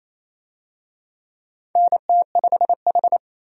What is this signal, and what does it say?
Morse code, heard as a keyed tone, about 27 wpm, 700 Hz. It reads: NT5H
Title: USA Call Signs - 4 Characters - 28wpm